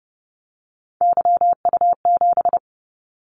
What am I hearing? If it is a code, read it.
Morse code, heard as a keyed tone, about 30 wpm, 700 Hz. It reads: YU7
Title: Callsign Prefixes for Top 25 DXCC - 30wpm